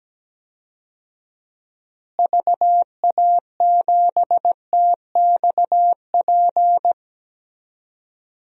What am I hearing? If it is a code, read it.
Morse code, heard as a keyed tone, about 17 wpm, 700 Hz. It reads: VA7TXP
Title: DXCC-100 Call Signs - 6 Characters - Repeated 2x - 17wpm